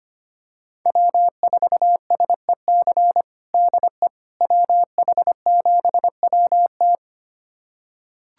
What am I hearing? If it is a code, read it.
Morse code, heard as a keyed tone, about 25 wpm, 700 Hz. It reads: W4SEC DE WH7WT
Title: USA Callsign DE Callsign - 5 Characters - 25wpm